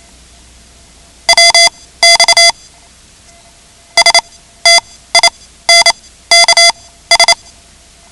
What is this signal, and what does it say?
Morse code, heard as a keyed tone, about 28 wpm, spaced as for 15 wpm, 2050 Hz. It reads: WX STINKS
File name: ham radio / CodeQuick / Code Quick 10